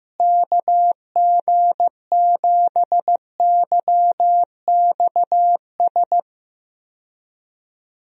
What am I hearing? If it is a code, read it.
Morse code, heard as a keyed tone, about 15 wpm, 700 Hz. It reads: KG7YXS
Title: USA Call Signs - 6 Characters - Repeated 3x - 15wpm